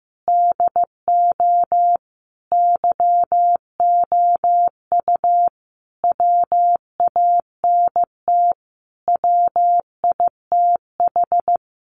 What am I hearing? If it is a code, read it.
Morse code, heard as a keyed tone, about 15 wpm, 700 Hz. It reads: DO YOU WANT WITH
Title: Sentences from Top 200 Words 15wpm